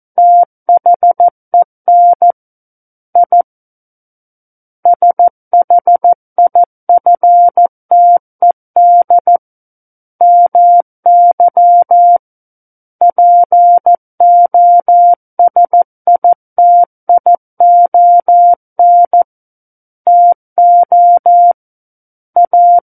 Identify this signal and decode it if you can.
Morse code, heard as a keyed tone, about 14 wpm, 700 Hz. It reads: THEN I SHIFTED MY POSITION TO A
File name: Book One / 05-Chapter 5 - 14 WPM